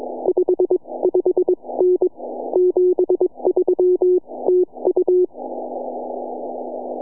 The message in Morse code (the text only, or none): 55N 73 TU